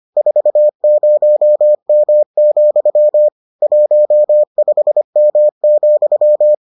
Morse code, heard as a keyed tone, about 25 wpm, 600 Hz. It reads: 40M, 15M,